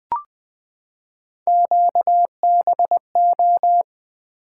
Morse code, heard as a keyed tone, about 20 wpm, 700 Hz. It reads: QBO
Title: Three Letter-Number Combinations - 20wpm